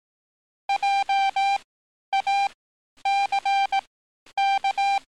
Morse code, heard as a keyed tone, about 18 wpm, spaced as for 10 wpm, 790 Hz. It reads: JACK
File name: training / NAVPERS 11049to11060 / NAVPERS 11050 RA RB